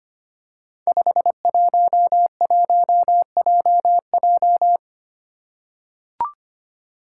Morse code, heard as a keyed tone, about 25 wpm, 700 Hz. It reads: 511JJ
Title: Five Challenging Letter-Number Combinations 25wpm